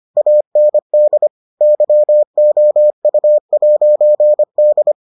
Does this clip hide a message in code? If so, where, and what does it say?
Yes: Morse code from 0.2 to 4.9 s, heard as a keyed tone, about 25 wpm, 600 Hz. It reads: AND YOU'D